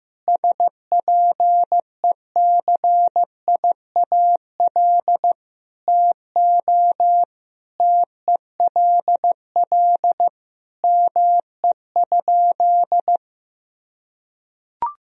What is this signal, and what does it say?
Morse code, heard as a keyed tone, about 15 wpm, 700 Hz. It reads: SPECIAL TO TELL ME?
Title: Sentences from Top 1000 Words - Review Part 3 - 15wpm